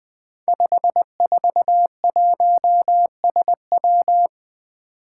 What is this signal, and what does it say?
Morse code, heard as a keyed tone, about 20 wpm, 700 Hz. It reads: 541SW
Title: Five Challenging Letter-Number Combinations 20wpm